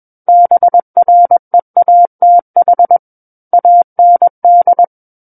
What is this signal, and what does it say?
Morse code, heard as a keyed tone, about 21 wpm, 700 Hz. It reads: BREATH AND
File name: Book One / 12-Chapter 12 - 21 WPM